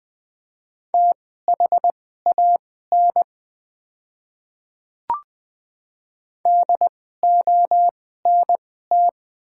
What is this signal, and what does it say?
Morse code, heard as a keyed tone, about 20 wpm, spaced as for 14 wpm, 700 Hz. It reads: THAN DONT
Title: 2x Character Spacing - Top 300 Words - 20wpm